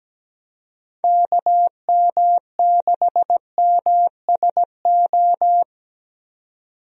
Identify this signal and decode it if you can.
Morse code, heard as a keyed tone, about 17 wpm, 700 Hz. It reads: KM6MSO
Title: USA Call Signs - 6 Characters - 17wpm